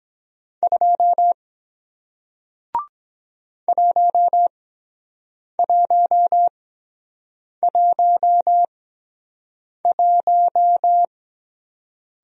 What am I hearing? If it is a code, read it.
Morse code, heard as a keyed tone, about 23 wpm, spaced as for 7 wpm, 700 Hz. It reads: 2 1111